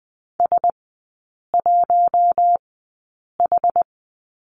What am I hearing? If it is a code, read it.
Morse code, heard as a keyed tone, about 20 wpm, spaced as for 8 wpm, 700 Hz. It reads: S1H
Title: 100 Lesson 17e - Letter u - Three Characters 20wpm